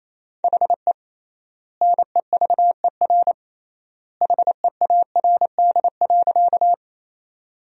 Morse code, heard as a keyed tone, about 28 wpm, 700 Hz. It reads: HE NEVER HEARD.
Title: 3x Word-Spacing - Sentences from Top 400 Words - 28wpm